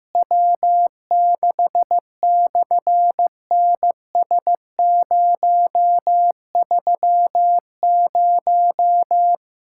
Morse code, heard as a keyed tone, about 15 wpm, 700 Hz. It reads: W6/NS030
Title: SOTA QSO Practice 15wpm